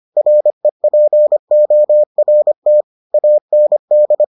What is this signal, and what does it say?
Morse code, heard as a keyed tone, about 25 wpm, 600 Hz. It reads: REPORT AND